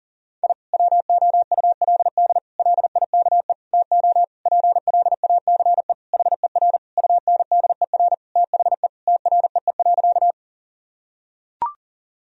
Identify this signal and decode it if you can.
Morse code, heard as a keyed tone, about 37 wpm, 700 Hz. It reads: I WOULD LIKE TO PLACE HER UNDER THE TREE.
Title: Sentences from Top 1000 Words - Review Part 1 - 40wpm